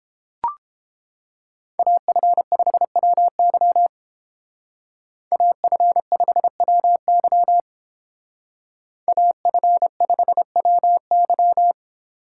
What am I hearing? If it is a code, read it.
Morse code, heard as a keyed tone, about 30 wpm, 700 Hz. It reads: AF5WY AF5WY AF5WY